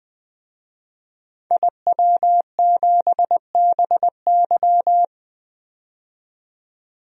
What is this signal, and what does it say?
Morse code, heard as a keyed tone, about 20 wpm, 700 Hz. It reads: IW7BY
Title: DXCC-100 Call Signs - 5 Characters - 20wpm